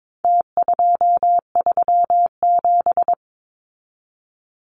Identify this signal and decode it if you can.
Morse code, heard as a keyed tone, about 22 wpm, 700 Hz. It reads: T237
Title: Numbers - Contest Style - 4-Digit 22wpm